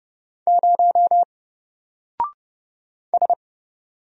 Morse code, heard as a keyed tone, about 30 wpm, 700 Hz. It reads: 0 S
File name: Single-Letter-Number-30wpm